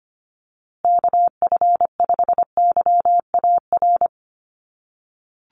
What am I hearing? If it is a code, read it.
Morse code, heard as a keyed tone, about 25 wpm, 700 Hz. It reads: KF5YAR
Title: USA Call Signs Repeated 3x 25wpm